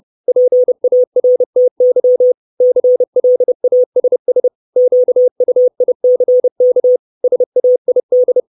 Morse code, heard as a keyed tone, about 30 wpm, 500 Hz. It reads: PARTY CLASS QUICK SAID
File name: 632 Words 30WPM